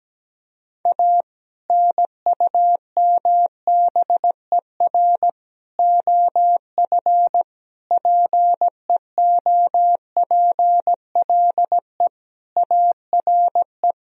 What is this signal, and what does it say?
Morse code, heard as a keyed tone, about 17 wpm, 700 Hz. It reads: A NUMBER OF PEOPLE ARE
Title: Sentences from Top 1000 Words - Review Part 1 - 17wpm